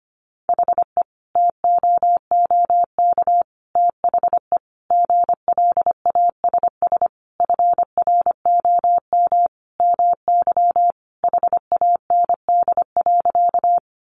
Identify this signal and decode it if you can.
Morse code, heard as a keyed tone, about 25 wpm, 700 Hz. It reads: HE TOOK THE GLASS FROM MY HAND.
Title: Sentences from Top 1300 Words 25wpm